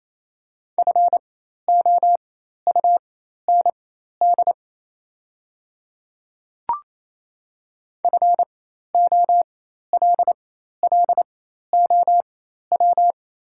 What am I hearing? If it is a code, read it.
Morse code, heard as a keyed tone, about 28 wpm, spaced as for 13 wpm, 700 Hz. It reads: FOUND FOLLOW
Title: 4x Character Spacing - Top 500 Words - 28wpm